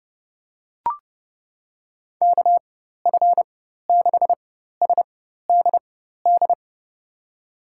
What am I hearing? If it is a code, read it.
Morse code, heard as a keyed tone, about 32 wpm, spaced as for 14 wpm, 700 Hz. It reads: KF6SDD
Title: USA Call Signs - 6 Characters - 4x Character Spacing - 30wpm